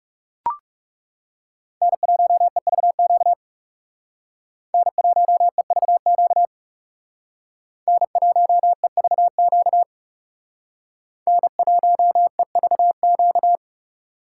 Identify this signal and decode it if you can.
Morse code, heard as a keyed tone, about 38 wpm, 700 Hz. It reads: N1EVQ N1EVQ N1EVQ N1EVQ